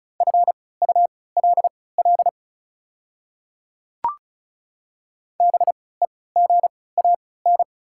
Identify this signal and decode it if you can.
Morse code, heard as a keyed tone, about 36 wpm, spaced as for 20 wpm, 700 Hz. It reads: FULL BEGAN
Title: 3x Character Spacing - Top 500 Words - 35wpm